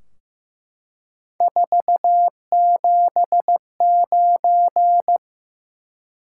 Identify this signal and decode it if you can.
Morse code, heard as a keyed tone, about 15 wpm, 700 Hz. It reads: 479
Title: Top 180 QSO Elements 15wpm